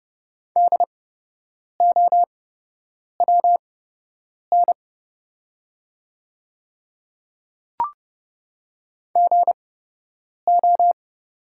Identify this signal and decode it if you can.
Morse code, heard as a keyed tone, about 31 wpm, spaced as for 8 wpm, 700 Hz. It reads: DOWN GO
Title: 8x Character Spacing - Top 100 Words - 30wpm